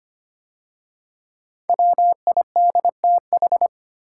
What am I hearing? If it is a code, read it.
Morse code, heard as a keyed tone, about 25 wpm, 700 Hz. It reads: WIDTH
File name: 5 Letter Words - First 3 Letters Given - 25wpm